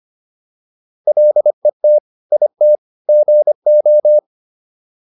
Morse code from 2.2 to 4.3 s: IT GO

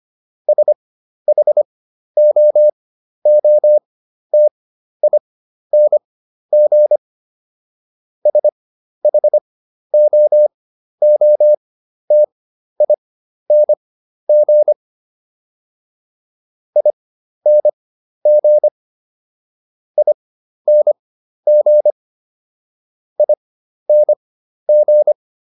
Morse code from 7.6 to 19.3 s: SHOOTING ING

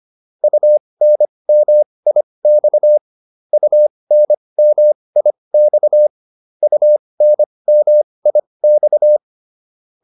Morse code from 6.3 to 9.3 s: UNMIX

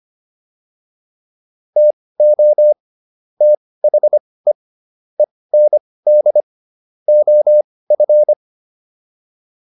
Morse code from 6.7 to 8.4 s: OF

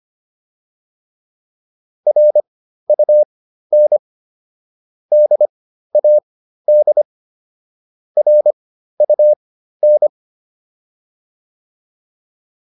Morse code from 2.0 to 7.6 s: RUN DAD

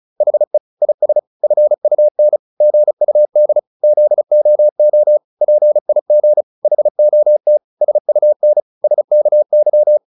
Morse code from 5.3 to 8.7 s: PIG HOT SUN

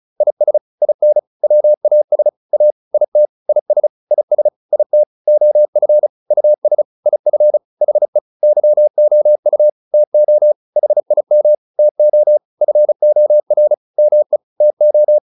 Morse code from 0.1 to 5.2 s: IS IN WAS A IT IS IS IT

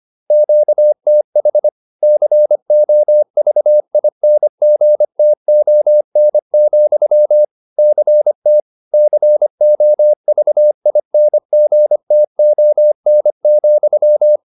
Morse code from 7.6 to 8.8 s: CT

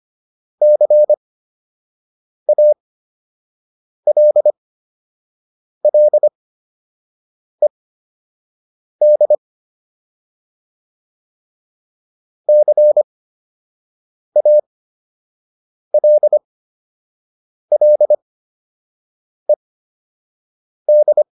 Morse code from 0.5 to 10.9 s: CALLED

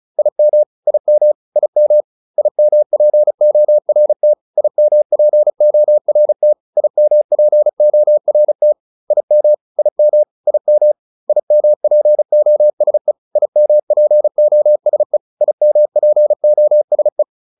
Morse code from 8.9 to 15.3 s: IM IM IM IMPOSE IMPOSE

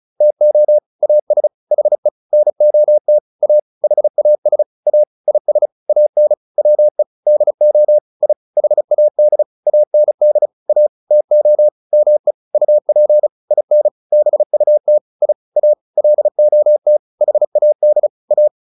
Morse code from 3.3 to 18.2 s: A HAS A IS AN WE DO I HAD AND A TO ME UP IN BUT I A LOT HAD

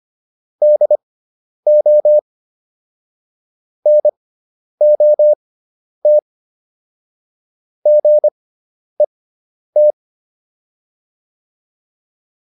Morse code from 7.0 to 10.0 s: GET

